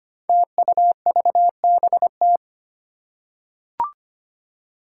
Morse code, heard as a keyed tone, about 25 wpm, 700 Hz. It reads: TUVBT
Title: Five Challenging Letter-Number Combinations 25wpm